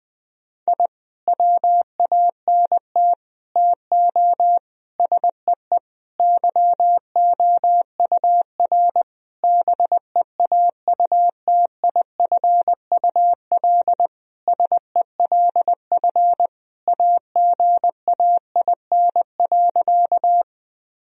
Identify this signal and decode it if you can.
Morse code, heard as a keyed tone, about 21 wpm, 700 Hz. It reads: I WANT TO SEE YOUR BEAUTIFUL SELF AGAIN.
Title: Sentences from Top 1000 Words - Review Part 1 - 20wpm